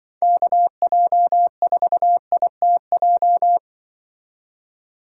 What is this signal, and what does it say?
Morse code, heard as a keyed tone, about 24 wpm, 700 Hz. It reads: KJ4ITJ
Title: Speed Racing - USA 2x3 Call Signs 20wpm